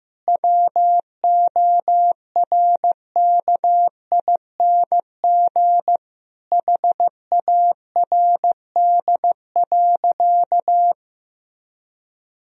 Morse code, heard as a keyed tone, about 15 wpm, 700 Hz. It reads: WORKING HARD.
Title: Sentences from Top 1000 Words - Review Part 4 - 15wpm